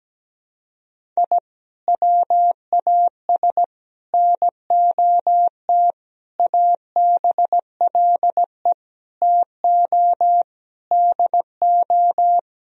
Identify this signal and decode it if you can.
Morse code, heard as a keyed tone, about 17 wpm, 700 Hz. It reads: I WAS NOT ABLE TO DO
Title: Sentences from Top 1000 Words - Review Part 2 - 17wpm